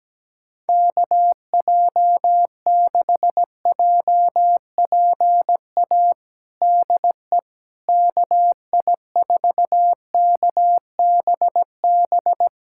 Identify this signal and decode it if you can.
Morse code, heard as a keyed tone, about 17 wpm, 700 Hz. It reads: KJ6JPA DE KI4KBB
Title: USA Callsign DE Callsign - 6 Characters - 17wpm